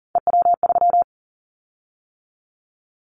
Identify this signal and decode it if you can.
Morse code, heard as a keyed tone, about 40 wpm, 700 Hz. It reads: EW3
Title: Callsign Prefixes for Top 50 DXCC - 40wpm